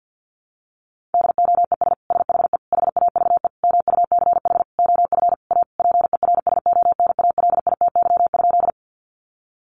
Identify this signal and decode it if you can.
Morse code, heard as a keyed tone, about 50 wpm, 700 Hz. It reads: DOES SHE HAVE MUCH OF A PERSONALITY?
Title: Sentences from Top 1400 Words 50wpm